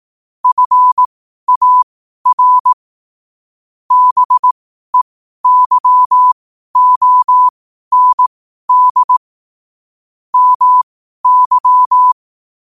Morse code, heard as a keyed tone, about 18 wpm, spaced as for 12 wpm, 1000 Hz. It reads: FAR BEYOND MY
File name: qotdfiles / 12wpm